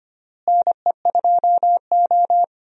Morse code, heard as a keyed tone, about 25 wpm, 700 Hz. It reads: NE2O
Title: USA Call Signs - 4 Characters - Repeated 3x - 25wpm